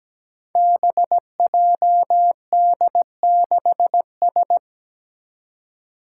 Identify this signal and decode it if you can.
Morse code, heard as a keyed tone, about 17 wpm, 700 Hz. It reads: BJD6S
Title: Five Challenging Letter-Number Combinations 17wpm